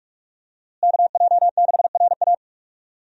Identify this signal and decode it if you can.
Morse code, heard as a keyed tone, about 46 wpm, 700 Hz. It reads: KJ6RA
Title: USA Call Signs - 5 Characters - Repeated 3x - 45wpm